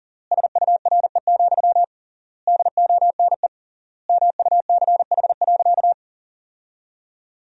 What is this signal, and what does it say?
Morse code, heard as a keyed tone, about 40 wpm, 700 Hz. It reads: SURE, DONE MUCH.
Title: 3x Word-Spacing - Sentences from Top 500 Words - 40wpm